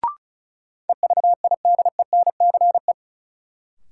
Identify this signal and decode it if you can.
Morse code, heard as a keyed tone, about 36 wpm, 700 Hz. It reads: EVIDENCE
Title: Sentences from Top 1300 Words - New Word Drill - 35wpm